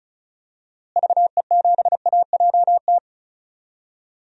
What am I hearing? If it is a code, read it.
Morse code, heard as a keyed tone, about 36 wpm, 700 Hz. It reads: VE7AJT
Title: DXCC-100 Call Signs - 6 Characters - Repeated 2x - 35wpm